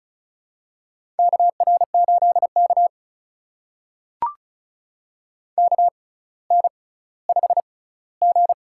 Morse code, heard as a keyed tone, about 36 wpm, spaced as for 18 wpm, 700 Hz. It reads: KR8K KN5G
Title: USA Call Signs - 4 Characters - 6x Character Spacing - 35wpm